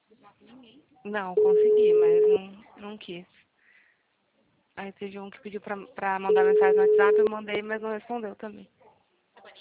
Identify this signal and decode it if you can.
Ringback tone (Europe, 425 Hz), with speech.
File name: out- -2012- - - .1630